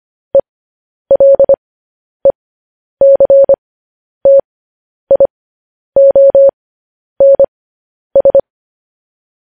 Morse code, heard as a keyed tone, about 25 wpm, spaced as for 10 wpm, 550 Hz. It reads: ELECTIONS
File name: podcast / News headlines-1x-25x10